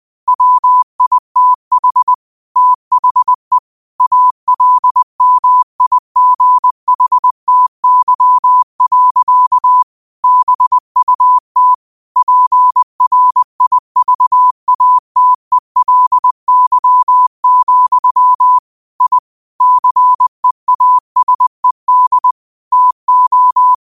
Morse code, heard as a keyed tone, about 20 wpm, 1000 Hz. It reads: WITH THE ALMIGHTY. BUT PRIVATELY, I CEASED TO
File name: qotdfiles / 20wpm